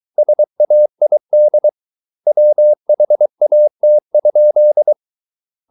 Morse code, heard as a keyed tone, about 23 wpm, 600 Hz. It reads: SAID WHAT?